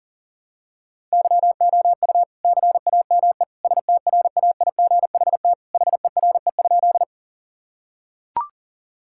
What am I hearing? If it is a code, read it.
Morse code, heard as a keyed tone, about 40 wpm, 700 Hz. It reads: YOU CAME STRAIGHT HERE?
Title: Sentences from Top 1000 Words - Review Part 6 - 40wpm